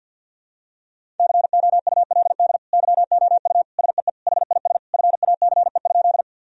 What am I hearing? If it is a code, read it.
Morse code, heard as a keyed tone, about 52 wpm, 700 Hz. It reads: COULD YOU SEE HIS FACE?
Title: Sentences from Top 1000 Words - Review Part 2 - 50wpm